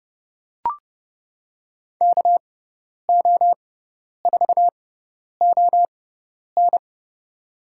Morse code, heard as a keyed tone, about 30 wpm, spaced as for 10 wpm, 700 Hz. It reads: KO4ON
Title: USA Call Signs - 6x Character Spacing - 30wpm